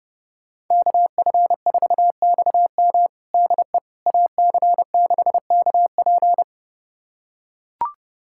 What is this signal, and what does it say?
Morse code, heard as a keyed tone, about 30 wpm, 700 Hz. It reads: KF4XM DE AC6KP
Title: USA Callsign DE Callsign - 5 Characters - 30wpm